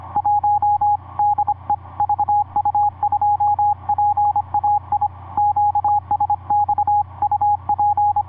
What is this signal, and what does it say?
Morse code, heard as a keyed tone, about 26 wpm, 830 Hz. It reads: 1 DE VU2PAI QSX UP
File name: vu2pai ik5pwq etc 17m cw 16jan2010 1003